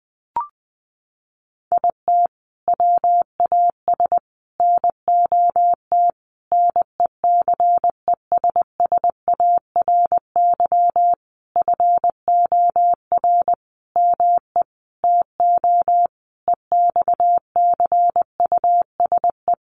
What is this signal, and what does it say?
Morse code, heard as a keyed tone, about 20 wpm, 700 Hz. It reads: IT WAS NOT NECESSARY FOR ME TO EXCUSE
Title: Sentences from Top 1300 Words 20wpm